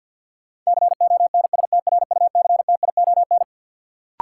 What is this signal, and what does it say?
Morse code, heard as a keyed tone, about 48 wpm, 700 Hz. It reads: CONSTRUCTION
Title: Sentences from Top 1600 Words - New Word Drill - 50wpm